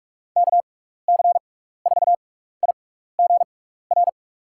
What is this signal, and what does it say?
Morse code, heard as a keyed tone, about 41 wpm, spaced as for 15 wpm, 700 Hz. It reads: KC4IGR
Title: USA Call Signs - 6 Characters - 6x Character Spacing - 45wpm